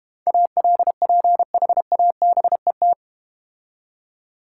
Morse code, heard as a keyed tone, about 32 wpm, 690 Hz. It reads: ALPHABET